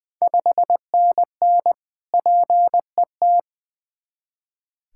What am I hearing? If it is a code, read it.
Morse code, heard as a keyed tone, about 20 wpm, 700 Hz. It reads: 5NN PET
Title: Ontario QSO Party Exchange 20wpm